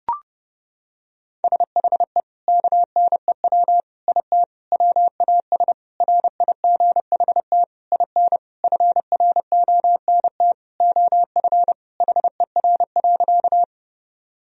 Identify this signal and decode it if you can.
Morse code, heard as a keyed tone, about 30 wpm, 700 Hz. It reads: SHE KNEW IT WAS RIGHT IN FRONT OF HER.
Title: Sentences from Top 1000 Words - Review Part 2 - 30wpm